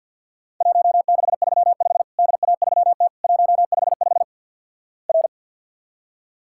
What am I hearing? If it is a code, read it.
Morse code, heard as a keyed tone, about 47 wpm, 700 Hz. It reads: 1635 DA3T 155 R
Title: Worked All Europe DX Contest - QTC Exchange - 50wpm